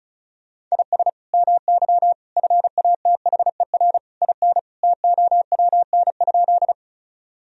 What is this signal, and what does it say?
Morse code, heard as a keyed tone, about 34 wpm, 700 Hz. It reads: IS MY FATHER IN TOWN?
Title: Sentences from Top 1000 Words - Review Part 2 - 35wpm